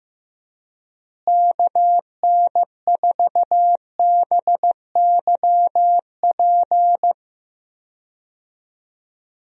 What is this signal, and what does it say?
Morse code, heard as a keyed tone, about 15 wpm, 700 Hz. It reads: KN4BYP
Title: USA Call Signs - 6 Characters - Repeated 2x - 15wpm